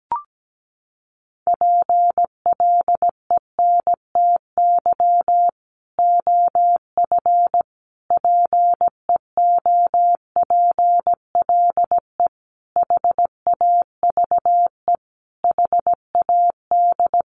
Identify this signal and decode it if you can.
Morse code, heard as a keyed tone, about 17 wpm, 700 Hz. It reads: PLENTY OF PEOPLE HAVE HAD
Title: Sentences from Top 1300 Words 17wpm